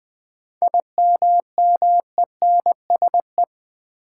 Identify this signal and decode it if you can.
Morse code, heard as a keyed tone, about 20 wpm, 700 Hz. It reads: IMMENSE
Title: 7 Letter Words - First 3 Letters Given - 20wpm